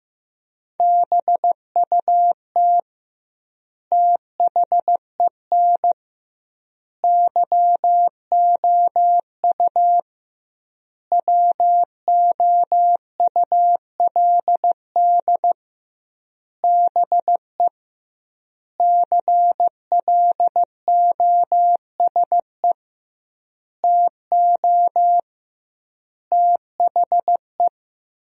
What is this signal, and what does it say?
Morse code, heard as a keyed tone, about 15 wpm, 700 Hz. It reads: BUT THEN YOU WOULD BE CLOSE TO THE
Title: 2x Word-Spacing - Sentences from Top 300 Words - 15wpm